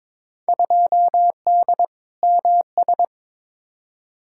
Morse code, heard as a keyed tone, about 22 wpm, 700 Hz. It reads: 2D MS
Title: Field Day Exchanges Encore - 22wpm